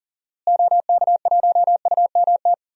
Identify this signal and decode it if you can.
Morse code, heard as a keyed tone, about 40 wpm, 700 Hz. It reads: OK1UMT
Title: DXCC-100 Call Signs - 6 Characters - Repeated 3x - 40wpm